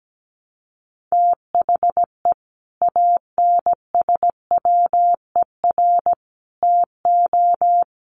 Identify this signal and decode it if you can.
Morse code, heard as a keyed tone, about 17 wpm, 700 Hz. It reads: THE ANSWER TO
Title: Sets of 5 Words Encore2 17wpm